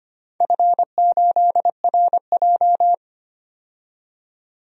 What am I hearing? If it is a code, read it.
Morse code, heard as a keyed tone, about 25 wpm, 700 Hz. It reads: F8RJ
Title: DXCC-100 Call Signs - 4 Characters - 25wpm